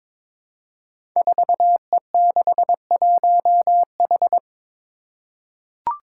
Morse code, heard as a keyed tone, about 22 wpm, 700 Hz. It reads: 4E61H
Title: Five Challenging Letter-Number Combinations 22wpm